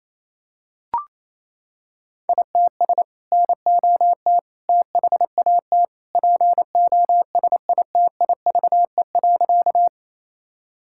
Morse code, heard as a keyed tone, about 28 wpm, 700 Hz. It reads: ITS NOT THAT POSITIVE.
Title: Sentences from Top 1000 Words - Review Part 5 - 28wpm